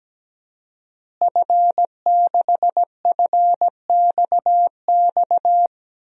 Morse code, heard as a keyed tone, about 17 wpm, 700 Hz. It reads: F6FXX
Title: DXCC-100 Call Signs - 5 Characters - Repeated 3x - 17wpm